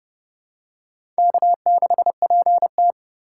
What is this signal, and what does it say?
Morse code, heard as a keyed tone, about 30 wpm, 700 Hz. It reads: K6PT
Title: USA Call Signs - 4 Characters - 4x Character Spacing - 30wpm